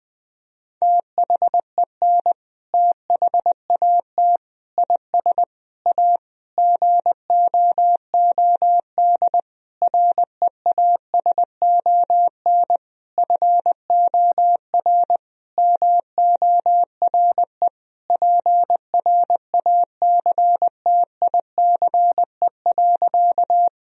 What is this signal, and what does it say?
Morse code, heard as a keyed tone, about 21 wpm, 700 Hz. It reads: THEN THAT IS A GOOD REASON FOR MORE PRACTICE.
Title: Sentences from Top 1000 Words - Review Part 4 - 20wpm